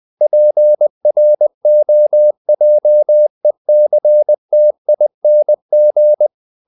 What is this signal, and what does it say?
Morse code, heard as a keyed tone, about 20 wpm, 600 Hz. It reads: PROJECTING
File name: the old man and the sea / the old man and the sea part 1 20wpm